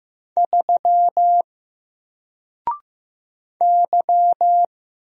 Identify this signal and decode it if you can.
Morse code, heard as a keyed tone, about 15 wpm, 700 Hz. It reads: 3 Y
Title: Single Letter-Number Alt 15wpm